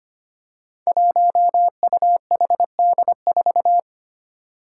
Morse code, heard as a keyed tone, about 25 wpm, 700 Hz. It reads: 1UHD4
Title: Five Challenging Letter-Number Combinations 25wpm